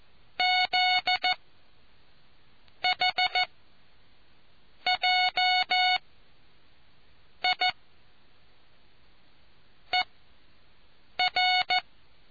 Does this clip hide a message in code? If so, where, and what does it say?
Yes: Morse code from 0.4 to 11.8 s, heard as a keyed tone, about 14 wpm, spaced as for 5 wpm, 730 Hz. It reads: ZH JIER